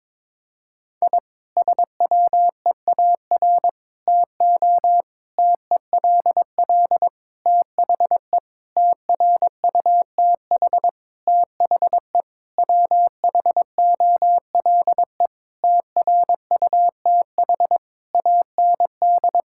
Morse code, heard as a keyed tone, about 22 wpm, 700 Hz. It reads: I SWEAR TO TELL THE TRUTH THE WHOLE TRUTH AND
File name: Sentences with 9-Word Phrases - 22wpm